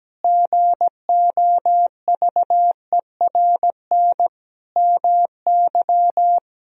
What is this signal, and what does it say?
Morse code, heard as a keyed tone, about 17 wpm, 700 Hz. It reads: GOVERN MY
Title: Sentences from Top 1000 Words - Review Part 2 - 17wpm